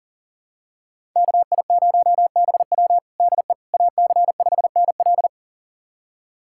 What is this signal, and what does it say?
Morse code, heard as a keyed tone, about 40 wpm, 700 Hz. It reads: KI0BW DE AC5NL
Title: USA Callsign DE Callsign - 5 Characters - 40wpm